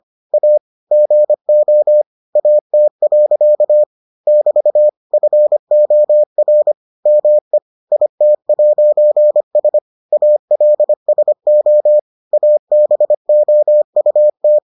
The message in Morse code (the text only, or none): A GO AT. <BT> FOR ME IT'S ALSO ABOUT